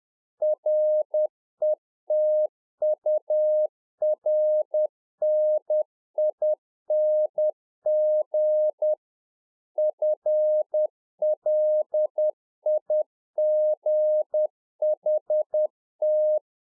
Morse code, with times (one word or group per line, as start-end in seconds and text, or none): 0.4-8.9 RETURNING
9.8-16.4 FLIGHT